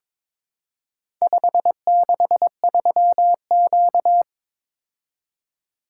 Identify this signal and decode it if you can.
Morse code, heard as a keyed tone, about 21 wpm, 700 Hz. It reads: 563Q
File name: practice / 22wpm / Four Letter-Number Combinations - 22wpm